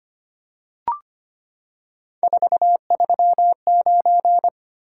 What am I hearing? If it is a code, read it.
Morse code, heard as a keyed tone, about 25 wpm, 700 Hz. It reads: 439
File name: Three Letter-Number Combinations - 25wpm